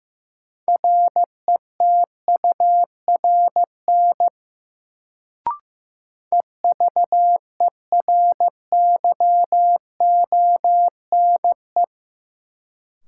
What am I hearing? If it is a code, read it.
Morse code, heard as a keyed tone, about 15 wpm, 700 Hz. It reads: RETURN EVERYONE
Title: Sentences from Top 800 Words - New Word Drill - 15wpm